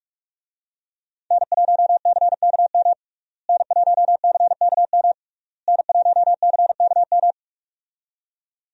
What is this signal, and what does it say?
Morse code, heard as a keyed tone, about 43 wpm, 700 Hz. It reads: N1CKM N1CKM N1CKM
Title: USA Call Signs - 5 Characters - Repeated 3x - 45wpm